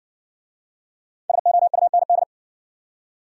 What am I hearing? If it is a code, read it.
Morse code, heard as a keyed tone, about 60 wpm, 700 Hz. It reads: SOUND